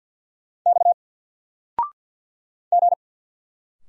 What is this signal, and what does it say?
Morse code, heard as a keyed tone, about 48 wpm, 700 Hz. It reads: X G
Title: Single Character (Letter:Number:Punctuation:Prosign) 50wpm